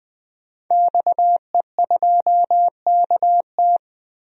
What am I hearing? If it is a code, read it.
Morse code, heard as a keyed tone, about 21 wpm, 700 Hz. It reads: XE2KT
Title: DXCC-100 Call Signs - 5 Characters - 20wpm